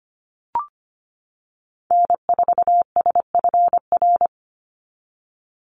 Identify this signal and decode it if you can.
Morse code, heard as a keyed tone, about 25 wpm, 700 Hz. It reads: N4SFR
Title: NA SOTA Activator Callsigns 25wpm